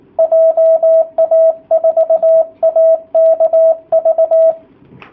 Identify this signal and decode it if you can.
Morse code, heard as a keyed tone, about 19 wpm, 660 Hz, with radio noise. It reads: JA4AKV
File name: jh8cbh / 2019cw / sp18- w